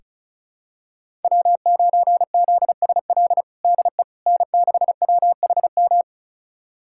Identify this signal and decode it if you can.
Morse code, heard as a keyed tone, about 36 wpm, 700 Hz. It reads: W9ZSL DE N6WHM
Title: USA Callsign DE Callsign - 5 Characters - 35wpm